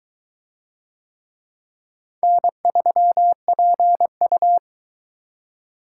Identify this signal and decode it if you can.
Morse code, heard as a keyed tone, about 22 wpm, 700 Hz. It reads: N3PU